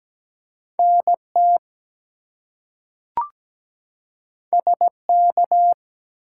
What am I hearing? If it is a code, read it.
Morse code, heard as a keyed tone, about 17 wpm, 700 Hz. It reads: NT SK
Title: US States and Canadian Provinces 17wpm